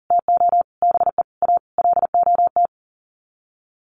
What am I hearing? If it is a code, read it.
Morse code, heard as a keyed tone, about 40 wpm, 700 Hz. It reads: TO BE A LOT
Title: Sets of 5 Words Encore 40wpm